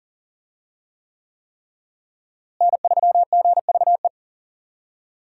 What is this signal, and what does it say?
Morse code, heard as a keyed tone, about 40 wpm, 700 Hz. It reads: N3GVE
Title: USA Call Signs - 5 Characters - Repeated 3x - 40wpm